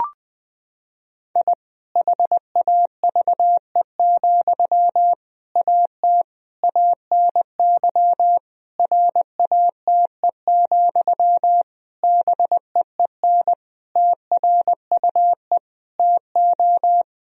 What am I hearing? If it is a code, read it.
Morse code, heard as a keyed tone, about 21 wpm, 700 Hz. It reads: I HAVE, AT ANY RATE, BEEN TRUE TO
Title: Sentences from Top 1000 Words - Review Part 4 - 20wpm